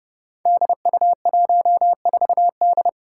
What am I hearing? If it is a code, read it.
Morse code, heard as a keyed tone, about 30 wpm, 700 Hz. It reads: DU14D
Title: Five Challenging Letter-Number Combinations 30wpm